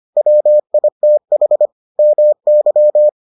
Morse code, heard as a keyed tone, about 25 wpm, 600 Hz. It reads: WITH MY